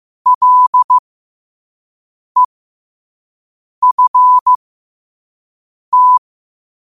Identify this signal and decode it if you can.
Morse code, heard as a keyed tone, about 15 wpm, spaced as for 5 wpm, 1000 Hz. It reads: LEFT
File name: qotdfiles / 5wpm